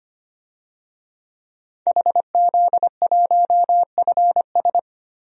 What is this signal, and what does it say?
Morse code, heard as a keyed tone, about 25 wpm, 700 Hz. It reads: HZ1FS
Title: DXCC-100 Call Signs - 5 Characters - 25wpm